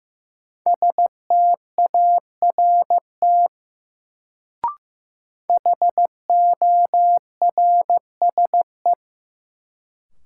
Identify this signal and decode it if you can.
Morse code, heard as a keyed tone, about 15 wpm, 700 Hz. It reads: START HORSE
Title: Sentences from Top 300 Words - New Word Drill - 15wpm